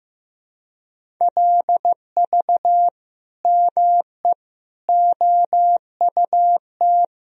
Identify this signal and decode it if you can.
Morse code, heard as a keyed tone, about 15 wpm, 700 Hz. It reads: LV ME OUT
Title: Sentences from Top 400 Words and Abbreviations 15wpm